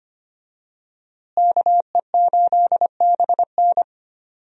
Morse code, heard as a keyed tone, about 25 wpm, 700 Hz. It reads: KE8BN
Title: USA Call Signs - 5 Characters - Repeated 3x - 25wpm